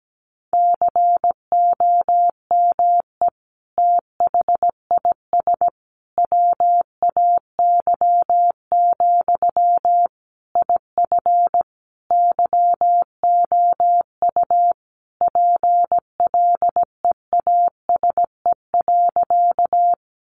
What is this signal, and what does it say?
Morse code, heard as a keyed tone, about 17 wpm, 700 Hz. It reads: COME THIS WAY, IF YOU PLEASE.
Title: Sentences from Top 1500 Words 17wpm